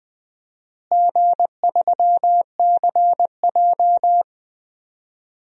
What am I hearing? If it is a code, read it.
Morse code, heard as a keyed tone, about 21 wpm, 700 Hz. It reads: G3CJ
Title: DXCC-100 Call Signs - 4 Characters - Repeated 3x - 20wpm